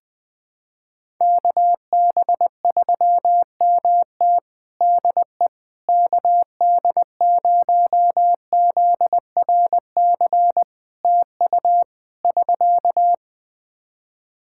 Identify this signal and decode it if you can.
Morse code, heard as a keyed tone, about 20 wpm, 700 Hz. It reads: KB3MT DE KD0ZRC TU <SK>
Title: Practice QSO - Exchange 4 - Ending - 20wpm